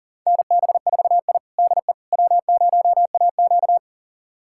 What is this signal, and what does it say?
Morse code, heard as a keyed tone, about 40 wpm, 700 Hz. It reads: NB4I DE W0AQ
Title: USA Callsign DE Callsign - 4 Characters - 40wpm